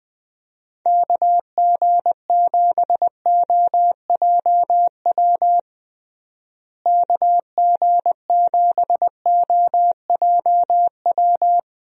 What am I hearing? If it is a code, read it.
Morse code, heard as a keyed tone, about 21 wpm, 700 Hz. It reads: KG7OJW KG7OJW
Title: USA Call Signs - 6 Characters - Repeated 2x - 20wpm